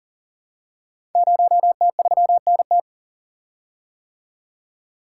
Morse code, heard as a keyed tone, about 40 wpm, 690 Hz. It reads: 0T3NT
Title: Code Groups - 5 Letter:Numbers - 40wpm